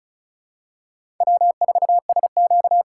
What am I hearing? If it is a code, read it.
Morse code, heard as a keyed tone, about 34 wpm, 700 Hz. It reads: W4SQ
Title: USA Call Signs - 4 Characters - 4x Character Spacing - 35wpm